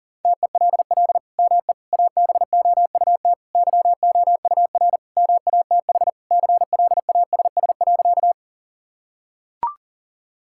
Morse code, heard as a keyed tone, about 40 wpm, 700 Hz. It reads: TELL ME ABOUT YOUR MATH CLASS.
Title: Sentences from Top 1000 Words - Review Part 6 - 40wpm